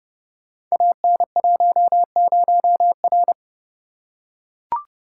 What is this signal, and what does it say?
Morse code, heard as a keyed tone, about 30 wpm, 700 Hz. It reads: AN10R
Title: Code Groups - 5 Letter:Numbers - 30wpm